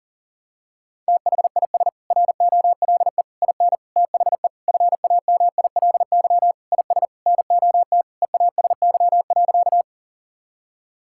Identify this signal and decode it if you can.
Morse code, heard as a keyed tone, about 37 wpm, 700 Hz. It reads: THIS ROLE IN THE FAMILY IS NOT EASY.
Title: Sentences from Top 1000 Words - Review Part 6 - 40wpm